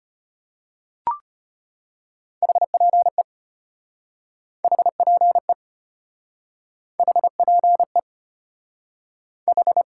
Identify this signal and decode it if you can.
Morse code, heard as a keyed tone, about 33 wpm, 700 Hz. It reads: HPE HPE HPE H